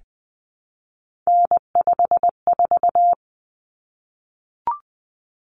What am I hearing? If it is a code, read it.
Morse code, heard as a keyed tone, about 20 wpm, 700 Hz. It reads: N54
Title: Common Cut Numbers - 3-Digit 20wpm